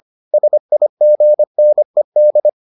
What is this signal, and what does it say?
Morse code, heard as a keyed tone, about 25 wpm, 600 Hz. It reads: SIGNED